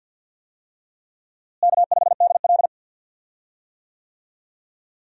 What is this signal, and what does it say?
Morse code, heard as a keyed tone, about 53 wpm, 700 Hz. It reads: K5DL
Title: USA Call Signs - 4 Characters - Repeated 3x - 50wpm